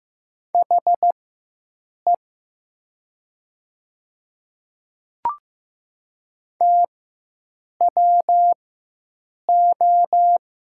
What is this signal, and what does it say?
Morse code, heard as a keyed tone, about 15 wpm, spaced as for 7 wpm, 700 Hz. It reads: HE TWO
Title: 4x Character Spacing - Top 100 Words - 15wpm